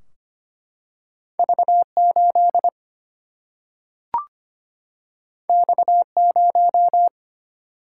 Morse code, heard as a keyed tone, about 25 wpm, 700 Hz. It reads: V 8 X 0
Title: Single-Letter Single-Number 25wpm